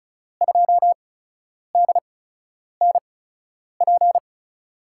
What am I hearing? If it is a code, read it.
Morse code, heard as a keyed tone, about 36 wpm, 700 Hz. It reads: 2 D N P
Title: USA Call Signs - 6 Characters - 8x Character Spacing - 35wpm